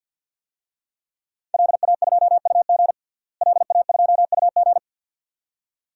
Morse code, heard as a keyed tone, about 47 wpm, 700 Hz. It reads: LA2UG LA2UG
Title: DXCC-100 Call Signs - 5 Characters - Repeated 2x - 50wpm